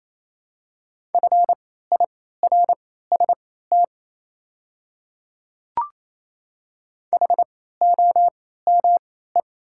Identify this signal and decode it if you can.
Morse code, heard as a keyed tone, about 28 wpm, spaced as for 16 wpm, 690 Hz. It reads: FIRST HOME
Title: 3x Character Spacing - Top 200 Words - 28wpm